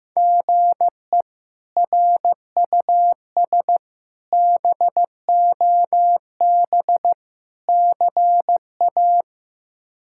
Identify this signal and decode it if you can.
Morse code, heard as a keyed tone, about 16 wpm, 700 Hz. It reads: GE RUS BOB CA
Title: K1USN SST QSO Practice - 15wpm